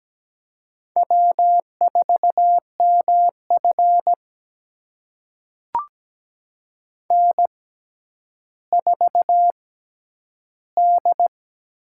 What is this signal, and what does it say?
Morse code, heard as a keyed tone, about 17 wpm, spaced as for 10 wpm, 700 Hz. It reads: W4MF N4D
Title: USA Call Signs - 6x Character Spacing - 17wpm